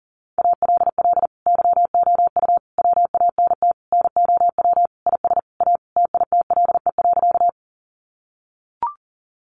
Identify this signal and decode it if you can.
Morse code, heard as a keyed tone, about 40 wpm, 700 Hz. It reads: ALL YOU WANT NOW IS A TITLE.
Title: Sentences from Top 1200 Words 40wpm